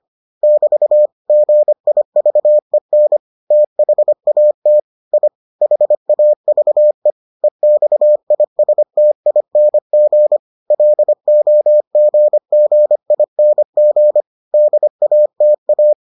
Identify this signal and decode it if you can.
Morse code, heard as a keyed tone, about 25 wpm, 600 Hz. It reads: <BT> GIVEN THAT I HAVE EXISTING LOGGING DATA